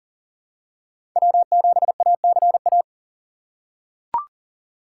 Morse code, heard as a keyed tone, about 40 wpm, 700 Hz. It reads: W7ACA
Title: USA Call Signs - 5 Characters - 8x Character Spacing - 40wpm